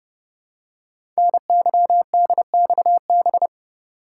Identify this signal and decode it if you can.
Morse code, heard as a keyed tone, about 29 wpm, 700 Hz. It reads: NYDXB
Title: Code Groups - 5 Letters - 30wpm